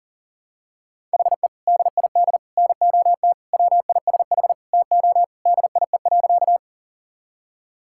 Morse code, heard as a keyed tone, about 40 wpm, 700 Hz. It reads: HE DID NOT WISH TO DIE.
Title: Sentences from Top 1600 Words 40wpm